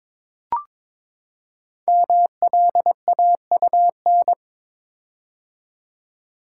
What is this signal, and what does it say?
Morse code, heard as a keyed tone, about 22 wpm, 690 Hz. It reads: MLAUN
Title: Code Groups - 5 Letters - 22wpm